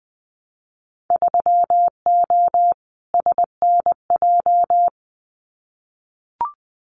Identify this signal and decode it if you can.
Morse code, heard as a keyed tone, about 20 wpm, 700 Hz. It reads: 3O SNJ
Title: Winter Field Day Exchanges Encore 20wpm